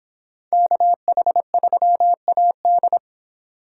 Morse code, heard as a keyed tone, about 26 wpm, 700 Hz. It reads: KH3AD
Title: Speed Racing - USA 2x2 Call Signs 22wpm